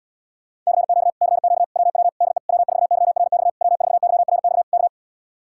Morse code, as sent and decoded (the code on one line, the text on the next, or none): -.-. --.- / -.-. --.- / .-- .-- / -.. / .-- ....- -.-- .-. -..- / .-- ....- -.-- .-. -..- / -.-
CQ CQ WW DE W4YRX W4YRX K